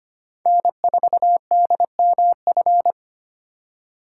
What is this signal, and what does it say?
Morse code, heard as a keyed tone, about 25 wpm, 700 Hz. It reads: N4DMF
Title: USA Call Signs - 5 Characters - 2x Character Spacing - 25wpm